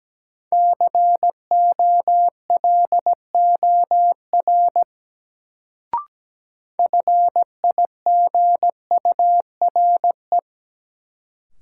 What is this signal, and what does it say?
Morse code, heard as a keyed tone, about 17 wpm, 700 Hz. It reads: COLOR FIGURE
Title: Sentences from Top 1000 Words - Review Part 2 - New Word Drill - 17wpm